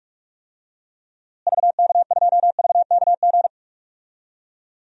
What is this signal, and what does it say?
Morse code, heard as a keyed tone, about 46 wpm, 700 Hz. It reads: VK2VKG
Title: DXCC-100 Call Signs - 6 Characters - Repeated 2x - 45wpm